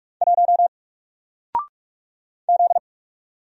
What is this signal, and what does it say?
Morse code, heard as a keyed tone, about 44 wpm, 700 Hz. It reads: 1 Z
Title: Single Letter-Number 45wpm